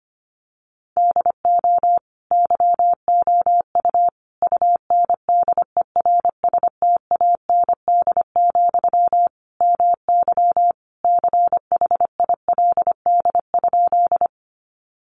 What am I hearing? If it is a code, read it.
Morse code, heard as a keyed tone, about 25 wpm, 700 Hz. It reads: DO YOU UNDERSTAND, MY CHILD?
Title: Sentences from Top 800 Words 25wpm